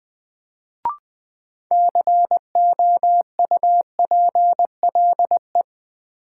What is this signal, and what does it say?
Morse code, heard as a keyed tone, about 20 wpm, 700 Hz. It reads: COUPLE
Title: Sets of 2 Words - New Word Drill - 20wpm